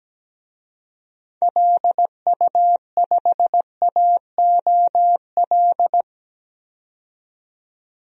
Morse code, heard as a keyed tone, about 17 wpm, 700 Hz. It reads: LU5AOL
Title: DXCC-100 Call Signs - 6 Characters - 17wpm